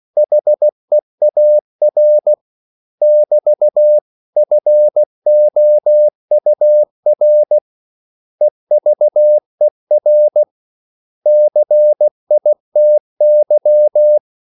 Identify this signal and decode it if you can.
Morse code, heard as a keyed tone, about 16 wpm, 600 Hz. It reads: HEAR <BT> FOUR EVER CITY
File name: CW / Practice / 500MostCommonEnglishWords16